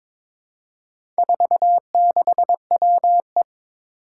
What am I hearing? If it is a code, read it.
Morse code, heard as a keyed tone, about 22 wpm, 700 Hz. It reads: 46WE
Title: Four Challenging Letter-Number Combinations 22wpm